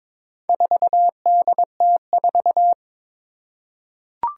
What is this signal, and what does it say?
Morse code, heard as a keyed tone, about 21 wpm, 700 Hz. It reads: 4DT4
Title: Four Challenging Letter-Number Combinations 22wpm